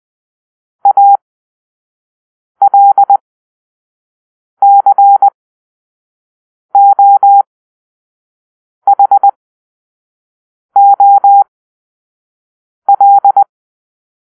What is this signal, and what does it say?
Morse code, heard as a keyed tone, about 20 wpm, spaced as for 5 wpm, 800 Hz. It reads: ALCOHOL
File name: Grp-3 Drill-1